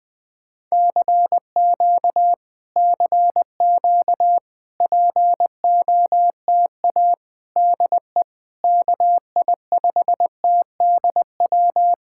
Morse code, heard as a keyed tone, about 21 wpm, 700 Hz. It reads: CQ CQ POTA DE KI5TDW
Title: POTA QSO Practice - 20wpm